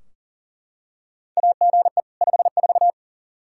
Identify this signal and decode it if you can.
Morse code, heard as a keyed tone, about 40 wpm, 700 Hz. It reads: AGE 54
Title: Practice QSO - Exchange 3 - More Info - 40wpm